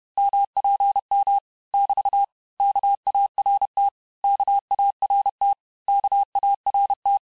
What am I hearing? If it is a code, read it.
Morse code, heard as a keyed tone, about 31 wpm, 800 Hz. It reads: MPM <BT> KART KART KART
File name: cours-du-mercredi / audios / cours-861-texte-les-mots-vitesse-29MPM